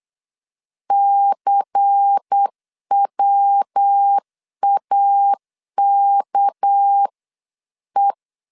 Morse code, heard as a keyed tone, about 8 wpm, 800 Hz. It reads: CWAKE
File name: LEZIONI / 8 YJ45 / 8 YJ 8wpm 3313